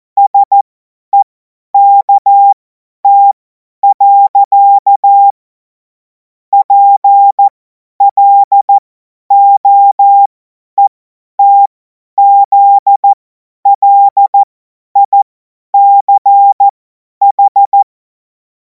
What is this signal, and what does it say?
Morse code, heard as a keyed tone, about 14 wpm, spaced as for 10 wpm, 800 Hz. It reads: SEKT. PLOETZLICH
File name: Morsekurs / Witze / Witz 14. April 2026 50 F70